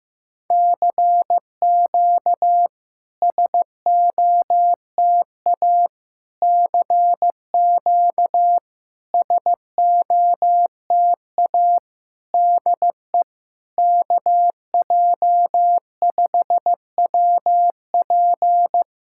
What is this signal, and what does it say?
Morse code, heard as a keyed tone, about 15 wpm, 700 Hz. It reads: CQ SOTA CQ SOTA DE KJ5WP
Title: SOTA QSO Practice 15wpm